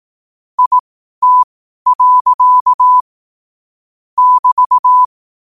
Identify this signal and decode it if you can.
Morse code, heard as a keyed tone, about 18 wpm, spaced as for 12 wpm, 1000 Hz. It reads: IT. <BT>
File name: qotdfiles / 12wpm